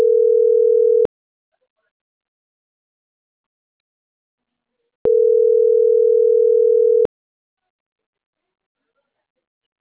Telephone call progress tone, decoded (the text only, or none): ringback tone